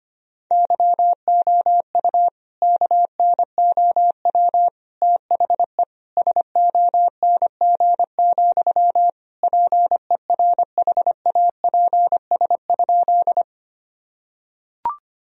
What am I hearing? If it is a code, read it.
Morse code, heard as a keyed tone, about 25 wpm, 700 Hz. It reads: YOU KNOW THE SONG, PERHAPS?
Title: Sentences from Top 1000 Words - Review Part 2 - 25wpm